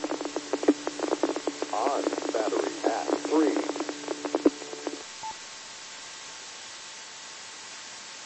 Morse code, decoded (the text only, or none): none